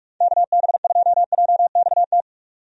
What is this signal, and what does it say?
Morse code, heard as a keyed tone, about 46 wpm, 700 Hz. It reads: KB2JXT
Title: USA Call Signs - 6 Characters - 45wpm